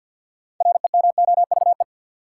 Morse code, heard as a keyed tone, about 52 wpm, 700 Hz. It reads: REMOVE
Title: Six Letter Words - 50wpm